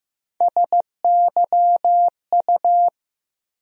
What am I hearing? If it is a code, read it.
Morse code, heard as a keyed tone, about 15 wpm, 700 Hz. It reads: SYU
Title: USA Call Sign Postfixes - 3 Letters 15wpm